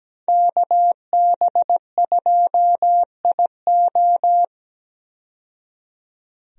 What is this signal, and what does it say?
Morse code, heard as a keyed tone, about 17 wpm, 700 Hz. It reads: KB2IO
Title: USA Call Signs - 5 Characters - 17wpm